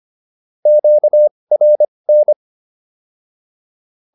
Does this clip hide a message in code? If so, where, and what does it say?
Yes: Morse code from 0.7 to 2.3 s, heard as a keyed tone, about 25 wpm, spaced as for 20 wpm, 600 Hz. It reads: QRN